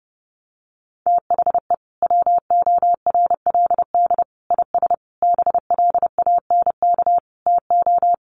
Morse code, heard as a keyed tone, about 30 wpm, 700 Hz. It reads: THE WORLD IS BLANK TO
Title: Sentences from Top 1500 Words 30wpm